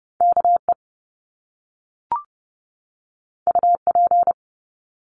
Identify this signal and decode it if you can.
Morse code, heard as a keyed tone, about 30 wpm, 700 Hz. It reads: KE UP
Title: Top Two Letter Combinations in Words 30wpm